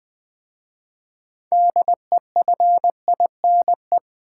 Morse code, heard as a keyed tone, about 20 wpm, 700 Hz. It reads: DEFINE
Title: 6 Letter Words - First 3 Letters Given - 20wpm